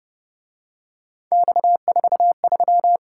Morse code, heard as a keyed tone, about 29 wpm, 700 Hz. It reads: X43